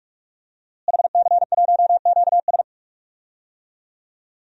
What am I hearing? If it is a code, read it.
Morse code, heard as a keyed tone, about 45 wpm, 700 Hz. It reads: HC1QS
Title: DXCC-100 Call Signs - 5 Characters - Repeated 3x - 45wpm